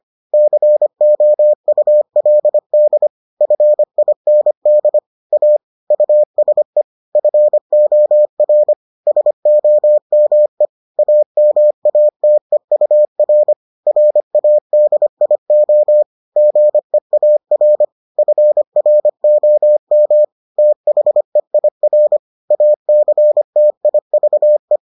Morse code, heard as a keyed tone, about 25 wpm, 600 Hz. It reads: COULD FIND A USE FOR SOME AMATEUR RADIO GEAR FROM THEIR ACTIVE